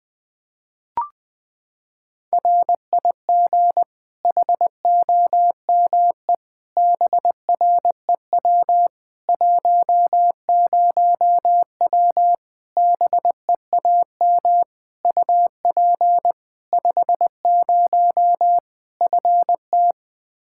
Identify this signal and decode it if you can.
Morse code, heard as a keyed tone, about 21 wpm, 700 Hz. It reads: RIG HOME BREW 10W BEAM UP 50 FT
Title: Practice QSO - Exchange 2 - Info - 20wpm